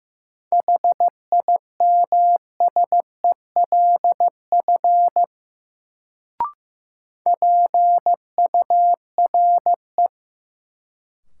Morse code, heard as a keyed tone, about 15 wpm, 700 Hz. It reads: HIMSELF PURE
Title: Sentences from Top 1500 Words - New Word Drill Alt - 15wpm